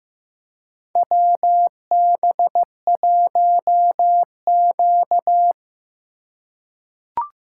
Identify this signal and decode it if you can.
Morse code, heard as a keyed tone, about 15 wpm, 700 Hz. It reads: WB1Q
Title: USA Call Signs - 3x Character Spacing - 15wpm